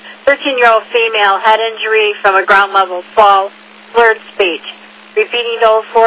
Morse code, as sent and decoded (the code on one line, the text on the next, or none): none